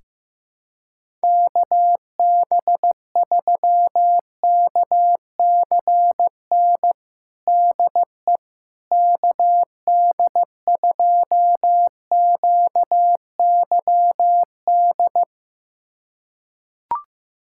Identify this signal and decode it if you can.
Morse code, heard as a keyed tone, about 16 wpm, 700 Hz. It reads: KB3KCN DE KD2QYD
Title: USA Callsign DE Callsign - 6 Characters - 15wpm